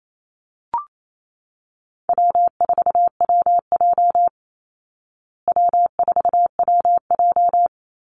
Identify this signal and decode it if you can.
Morse code, heard as a keyed tone, about 28 wpm, 700 Hz. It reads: W4WJ W4WJ
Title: Easy Call Signs Repeated 2x - 28wpm